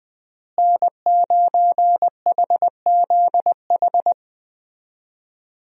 Morse code, heard as a keyed tone, about 19 wpm, 700 Hz. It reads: N9HZH
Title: USA Call Signs - 5 Characters - Repeated 3x - 20wpm